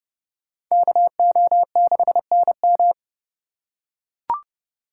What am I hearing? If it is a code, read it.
Morse code, heard as a keyed tone, about 30 wpm, 700 Hz. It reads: KO6NM